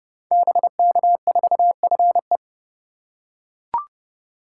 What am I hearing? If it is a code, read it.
Morse code, heard as a keyed tone, about 30 wpm, 700 Hz. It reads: BK4FE
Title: Code Groups - 5 Letter:Numbers - 30wpm